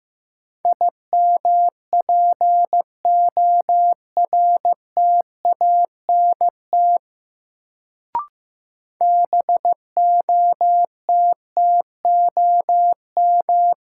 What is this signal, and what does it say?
Morse code, heard as a keyed tone, about 15 wpm, 700 Hz. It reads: IMPORTANT BOTTOM
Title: Sets of 3 Words - New Word Drill - 15wpm